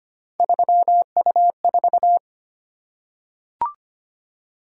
Morse code, heard as a keyed tone, about 25 wpm, 700 Hz. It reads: 3U4
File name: Three Letter-Number Combinations - 25wpm